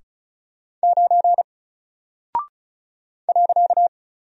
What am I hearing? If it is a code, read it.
Morse code, heard as a keyed tone, about 36 wpm, 700 Hz. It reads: 9 .
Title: Single Character (Letter:Number:Punctuation:Prosign) Alt 35wpm